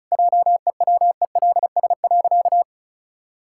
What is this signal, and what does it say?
Morse code, heard as a keyed tone, about 34 wpm, 700 Hz. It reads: JEWELS.
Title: Pangram Challenge v2 - 35wpm